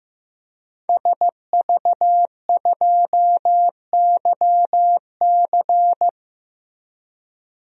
Morse code, heard as a keyed tone, about 16 wpm, 700 Hz. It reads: SV2YC
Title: DXCC-100 Call Signs - 5 Characters - Repeated 3x - 15wpm